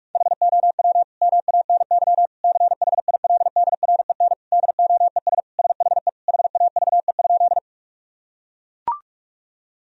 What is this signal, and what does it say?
Morse code, heard as a keyed tone, about 44 wpm, 700 Hz. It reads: HOW MANY CHILDREN DOES SHE HAVE?
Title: Sentences from Top 1000 Words - Review Part 1 - 45wpm